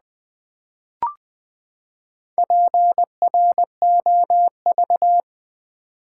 Morse code, heard as a keyed tone, about 20 wpm, 700 Hz. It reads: PROV
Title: Build A Word - 4 Letters At A Time - 10-Letter Words - 20wpm